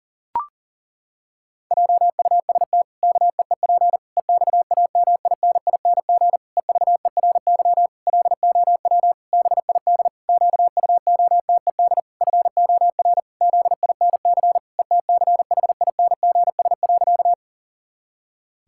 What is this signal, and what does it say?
Morse code, heard as a keyed tone, about 40 wpm, 700 Hz. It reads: JUST KEEP EXAMINING EVERY LOW BID QUOTED FOR ZINC ETCHINGS.
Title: Pangram Challenge v2 - 40wpm